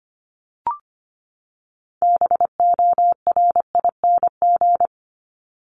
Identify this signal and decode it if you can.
Morse code, heard as a keyed tone, about 25 wpm, 700 Hz. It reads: BORING
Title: Top 1501-1750 Words 25wpm